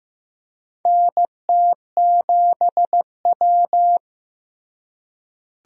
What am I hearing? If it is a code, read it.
Morse code, heard as a keyed tone, about 16 wpm, 700 Hz. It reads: NT7W
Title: USA Call Signs - 4 Characters - Repeated 2x - 15wpm